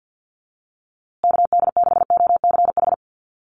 Morse code, heard as a keyed tone, about 50 wpm, 700 Hz. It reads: KD6OCH
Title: USA Call Signs Repeated 3x 50wpm